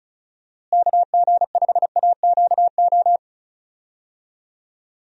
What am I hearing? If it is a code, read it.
Morse code, heard as a keyed tone, about 34 wpm, 700 Hz. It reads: KG5AQO
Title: USA Call Signs - 6 Characters - Repeated 3x - 35wpm